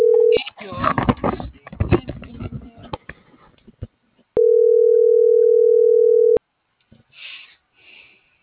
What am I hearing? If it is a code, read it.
Ringback tone (North America), 440 + 480 Hz.